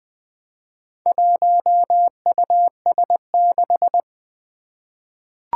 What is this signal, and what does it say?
Morse code, heard as a keyed tone, about 20 wpm, 700 Hz. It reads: 1US6
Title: Four Challenging Letter-Number Combinations 20wpm